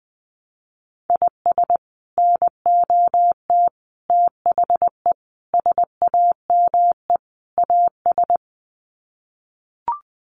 Morse code, heard as a keyed tone, about 20 wpm, 700 Hz. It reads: IS NOT THE SAME AS
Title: Sets of 5 Words Encore 20wpm